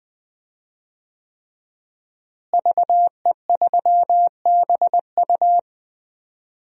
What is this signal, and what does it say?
Morse code, heard as a keyed tone, about 20 wpm, 700 Hz. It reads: VE3BU
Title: DXCC-100 Call Signs - 5 Characters - Repeated 3x - 20wpm